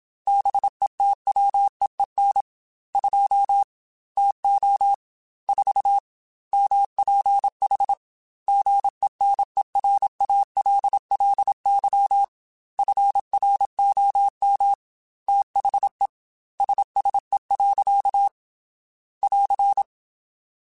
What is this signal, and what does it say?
Morse code, heard as a keyed tone, about 27 wpm, 800 Hz. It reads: BETWEEN 2 TO 4 MPH GENERALLY FROM THE SSE. <AR>